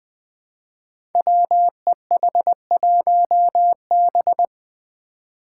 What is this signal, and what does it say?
Morse code, heard as a keyed tone, about 20 wpm, 700 Hz. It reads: WEH1B
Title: Five Challenging Letter-Number Combinations 20wpm